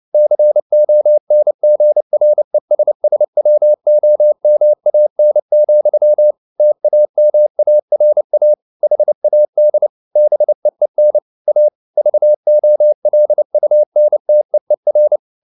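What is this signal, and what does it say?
Morse code, heard as a keyed tone, about 29 wpm, 600 Hz. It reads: CONGRESSWOMAN, TAMARA HAD BEEN A VOLUNTEER
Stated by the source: Morse code at 29 wpm, sending Ken Follett's novel Never in English, from CW med Gnister.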